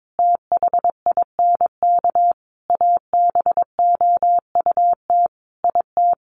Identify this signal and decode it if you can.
Morse code, heard as a keyed tone, about 22 wpm, 700 Hz. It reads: THINK ABOUT IT
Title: Sets of 5 Words Encore2 22wpm